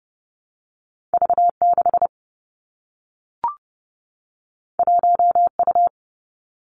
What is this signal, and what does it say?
Morse code, heard as a keyed tone, about 30 wpm, 700 Hz. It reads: V 6 1 U
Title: Two Challenging Letter-Number Combinations 30wpm